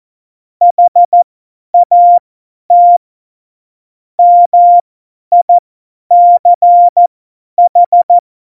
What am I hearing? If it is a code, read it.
Morse code, heard as a keyed tone, about 14 wpm, spaced as for 10 wpm, 700 Hz. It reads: HAT MICH
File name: Morsekurs / Witze / Witz 01. Mai 2018, 50 F70